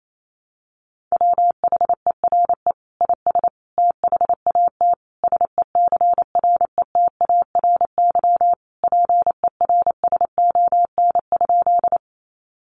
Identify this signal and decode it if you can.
Morse code, heard as a keyed tone, about 28 wpm, 700 Hz. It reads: WHERE IS THAT SECRETARY PERSON?
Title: Sentences from Top 1300 Words 28wpm